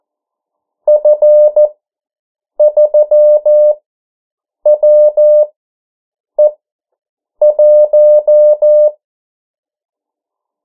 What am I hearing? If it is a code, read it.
Morse code, heard as a keyed tone, about 14 wpm, spaced as for 7 wpm, 600 Hz. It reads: F3WE1
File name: morse mp3 / 09 urok 8